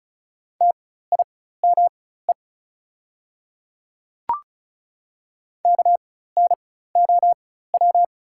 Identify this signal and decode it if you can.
Morse code, heard as a keyed tone, about 36 wpm, spaced as for 16 wpm, 700 Hz. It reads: TIME KNOW
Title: 4x Character Spacing - Top 100 Words - 35wpm